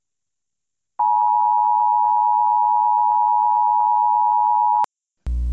Emergency Alert System (SAME) tones (the 853 and 960 Hz attention tone). It is heard from 0.9 to 4.9 s.
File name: Severe-T-Storm-Warning-Parking-Lots